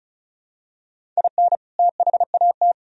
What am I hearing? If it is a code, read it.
Morse code, heard as a keyed tone, about 36 wpm, 700 Hz. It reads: IN THAT
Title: Sentences from Top 1000 Words - Review Part 1 - 35wpm